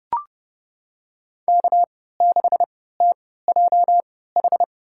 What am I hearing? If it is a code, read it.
Morse code, heard as a keyed tone, about 32 wpm, spaced as for 17 wpm, 700 Hz. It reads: K6TJH
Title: USA Call Signs - 5 Characters - 3x Character Spacing - 30wpm